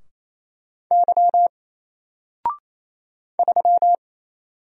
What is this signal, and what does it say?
Morse code, heard as a keyed tone, about 27 wpm, 700 Hz. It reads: Y 3
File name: Single-Letter-Number-28wpm